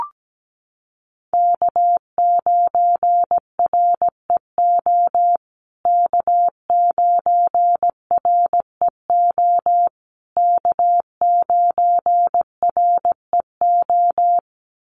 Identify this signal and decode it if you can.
Morse code, heard as a keyed tone, about 17 wpm, 700 Hz. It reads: K9REO K9REO K9REO
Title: USA Call Signs Repeated 3x 17wpm